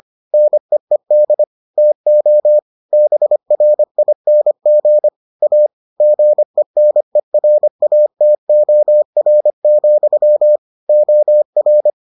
Morse code, heard as a keyed tone, about 25 wpm, 600 Hz. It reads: NEED TO BRING A GENERATOR, OR